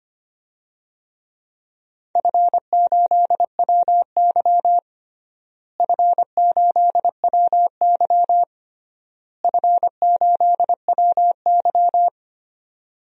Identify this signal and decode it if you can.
Morse code, heard as a keyed tone, about 25 wpm, 700 Hz. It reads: F8WY F8WY F8WY
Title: DXCC-100 Call Signs - 4 Characters - Repeated 3x - 25wpm